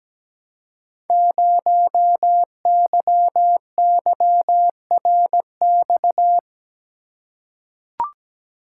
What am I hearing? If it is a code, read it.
Morse code, heard as a keyed tone, about 17 wpm, 700 Hz. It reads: 0YYRX
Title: Code Groups - 5 Letter:Numbers - 17wpm